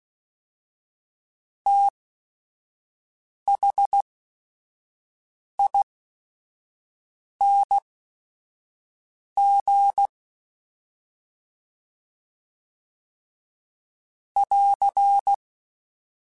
Morse code, heard as a keyed tone, about 16 wpm, spaced as for 5 wpm, 800 Hz. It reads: THING <AR>
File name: Special INGs / Special INGs-mp3s / SlowToFaster ing @5